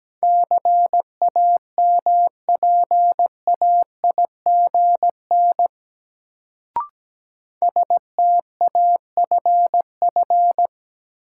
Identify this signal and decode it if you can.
Morse code, heard as a keyed tone, about 17 wpm, 700 Hz. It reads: CAMPAIGN STAFF
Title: Sentences from Top 1200 Words - New Word Drill Alt - 17wpm